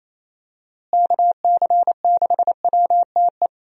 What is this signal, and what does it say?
Morse code, heard as a keyed tone, about 27 wpm, 700 Hz. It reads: KC6WTE
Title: USA Call Signs - 6 Characters - Repeated 2x - 28wpm